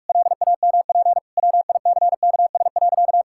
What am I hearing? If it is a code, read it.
Morse code, heard as a keyed tone, about 46 wpm, 700 Hz. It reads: LAMP WICKS.
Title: Pangram Challenge v2 - 45wpm